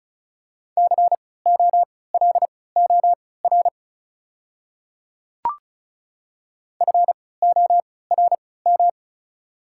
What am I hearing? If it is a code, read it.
Morse code, heard as a keyed tone, about 34 wpm, spaced as for 20 wpm, 700 Hz. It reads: COLOR FORM
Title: 3x Character Spacing - Top 300 Words - 35wpm